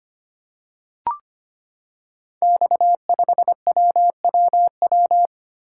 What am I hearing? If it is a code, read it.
Morse code, heard as a keyed tone, about 25 wpm, 700 Hz. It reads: X5WWW
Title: Code Groups - 5 Letter:Numbers - 25wpm